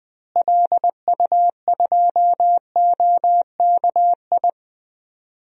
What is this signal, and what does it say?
Morse code, heard as a keyed tone, about 20 wpm, 700 Hz. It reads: LU2OKI
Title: DXCC-100 Call Signs - 6 Characters - Repeated 3x - 20wpm